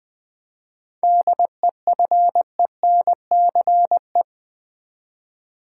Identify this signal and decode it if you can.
Morse code, heard as a keyed tone, about 20 wpm, 690 Hz. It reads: DEFENCE
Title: 7 Letter Words - First 3 Letters Given - 20wpm